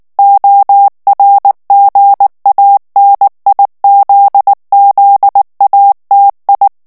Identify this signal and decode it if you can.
Morse code, heard as a keyed tone, about 19 wpm, 800 Hz. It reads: ORGANIZZATI
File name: morse mp3 / ITALIEN texte clair 2 vitesse 18 mots minute